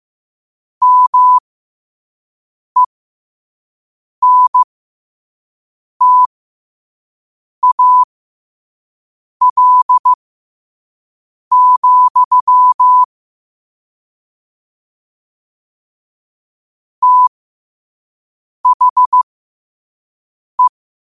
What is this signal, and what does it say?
Morse code, heard as a keyed tone, about 15 wpm, spaced as for 5 wpm, 1000 Hz. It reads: MENTAL, THE
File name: qotdfiles / 5wpm